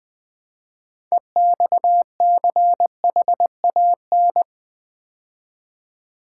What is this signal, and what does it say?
Morse code, heard as a keyed tone, about 20 wpm, 700 Hz. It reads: EXCHAN
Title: Build A Word - 3 Letters At A Time - 8-Letter Words - 20wpm